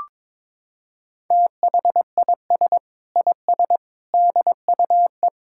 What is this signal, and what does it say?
Morse code, heard as a keyed tone, about 21 wpm, 700 Hz. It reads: THIS IS DUE
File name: practice / 22wpm / Sentences with 8-Word Phrases - 22wpm